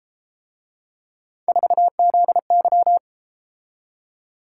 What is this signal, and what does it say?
Morse code, heard as a keyed tone, about 31 wpm, 700 Hz. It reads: 4ZY